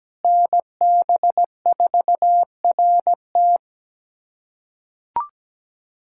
Morse code, heard as a keyed tone, about 17 wpm, 700 Hz. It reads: NB4RT
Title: USA Callsign DE Callsign 17wpm